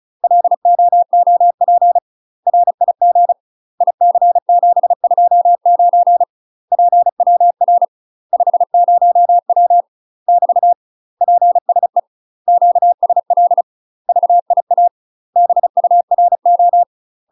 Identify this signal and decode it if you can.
Morse code, heard as a keyed tone, about 35 wpm, 700 Hz. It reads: LOOP RIG IC729 PWR 50W <BT> PSE QSL VIA BURO
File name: UK MASTER QSOs TRAINING/35WPM